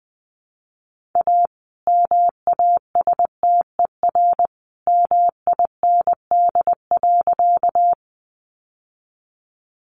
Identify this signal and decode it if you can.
Morse code, heard as a keyed tone, about 20 wpm, 700 Hz. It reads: A MASTER MIND.
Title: Sentences from Top 1300 Words 20wpm